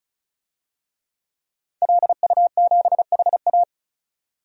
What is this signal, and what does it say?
Morse code, heard as a keyed tone, about 36 wpm, 700 Hz. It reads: LU7HA
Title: DXCC-100 Call Signs - 5 Characters - 35wpm